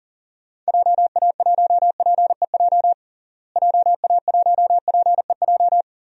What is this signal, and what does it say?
Morse code, heard as a keyed tone, about 40 wpm, 700 Hz. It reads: JA1PEJ JA1PEJ
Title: DXCC-100 Call Signs - 6 Characters - Repeated 3x - 40wpm